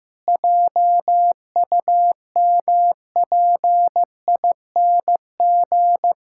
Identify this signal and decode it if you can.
Morse code, heard as a keyed tone, about 15 wpm, 700 Hz. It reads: JUMPING
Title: Pangram Challenge v2 - 15wpm